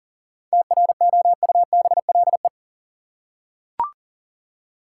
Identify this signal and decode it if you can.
Morse code, heard as a keyed tone, about 38 wpm, 700 Hz. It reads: TROUBLE
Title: Seven Letter Words - 40wpm